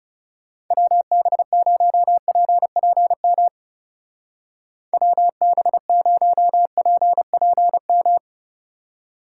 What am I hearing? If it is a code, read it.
Morse code, heard as a keyed tone, about 37 wpm, 700 Hz. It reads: WB0PPM WB0PPM
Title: Speed Racing - USA Call Signs 30wpm